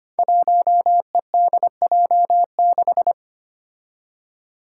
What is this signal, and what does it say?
Morse code, heard as a keyed tone, about 25 wpm, 700 Hz. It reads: 1EDJ6
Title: Five Challenging Letter-Number Combinations 25wpm